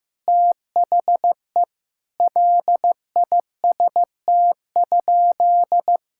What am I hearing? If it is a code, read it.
Morse code, heard as a keyed tone, about 15 wpm, 700 Hz. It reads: THE LIST?
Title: Sentences from Top 1000 Words - Review Part 6 - 15wpm